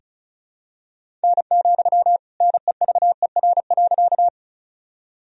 Morse code, heard as a keyed tone, about 36 wpm, 700 Hz. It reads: N, NEVER.
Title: Sentences from Top 300 Words and Abbreviations 35wpm